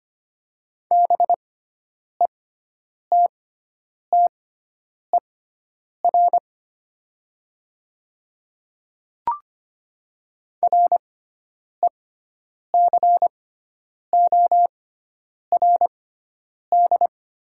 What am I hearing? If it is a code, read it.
Morse code, heard as a keyed tone, about 25 wpm, spaced as for 9 wpm, 700 Hz. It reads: BETTER RECORD
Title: 6x Character Spacing - Top 500 Words - 25wpm